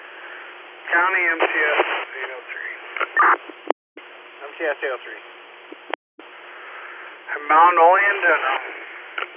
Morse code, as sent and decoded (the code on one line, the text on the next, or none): none